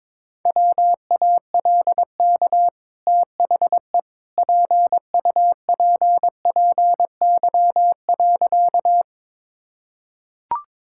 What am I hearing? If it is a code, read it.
Morse code, heard as a keyed tone, about 22 wpm, 690 Hz. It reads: WALK THE PUPPY.
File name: practice / 22wpm / Easy Sentences with Hints - 22wpm